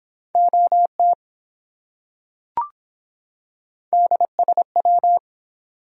Morse code, heard as a keyed tone, about 24 wpm, 700 Hz. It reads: OT DSW